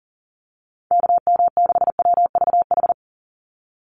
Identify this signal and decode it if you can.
Morse code, heard as a keyed tone, about 40 wpm, 700 Hz. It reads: KM6WVH
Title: USA Call Signs 40wpm